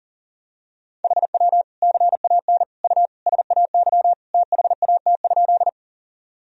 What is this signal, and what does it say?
Morse code, heard as a keyed tone, about 37 wpm, 700 Hz. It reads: HW CAN U SAY THAT?
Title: Sentences from Top 200 Words and Abbreviations 40wpm